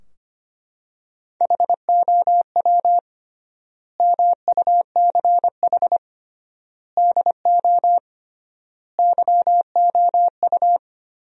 Morse code, heard as a keyed tone, about 25 wpm, 700 Hz. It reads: HOW MUCH DO YOU
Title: 3x Word-Spacing - Sentences from Top 200 Words - 25wpm